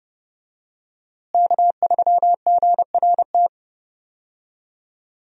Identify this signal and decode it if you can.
Morse code, heard as a keyed tone, about 30 wpm, 700 Hz. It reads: K3GRT
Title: USA Call Signs - 5 Characters - Repeated 3x - 30wpm